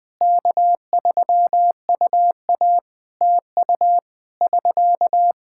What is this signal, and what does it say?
Morse code, heard as a keyed tone, about 20 wpm, 700 Hz. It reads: K3UA TU <SK>
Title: K1USN SST QSO Practice 20wpm